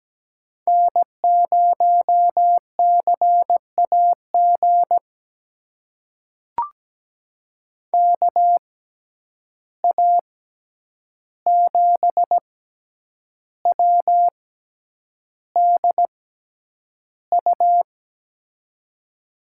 Morse code, heard as a keyed tone, about 17 wpm, spaced as for 8 wpm, 700 Hz. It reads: N0CAG KA7WDU
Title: USA Call Signs - 6x Character Spacing - 17wpm